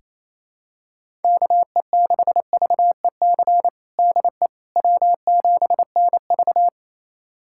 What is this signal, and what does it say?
Morse code, heard as a keyed tone, about 28 wpm, 700 Hz. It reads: KE6VEC DE W7NV
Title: USA Callsign DE Callsign 28wpm